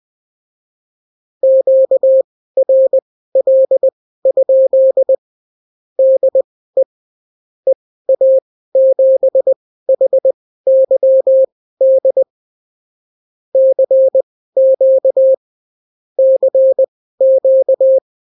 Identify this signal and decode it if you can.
Morse code, heard as a keyed tone, about 20 wpm, spaced as for 14 wpm, 530 Hz. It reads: QRL? DE EA7HYD CQ CQ
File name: CW QSO libroCW4 20-10ppm